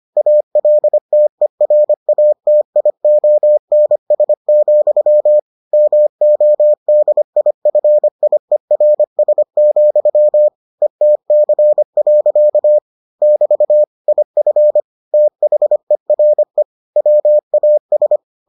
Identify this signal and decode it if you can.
Morse code, heard as a keyed tone, about 25 wpm, 600 Hz. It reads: ALTERATIONS, MODIFIERS, ETC. <BT> IF THERE WAS